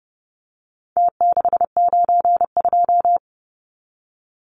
Morse code, heard as a keyed tone, about 30 wpm, 700 Hz. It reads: T692
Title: Numbers - Contest Style - 4-Digit 30wpm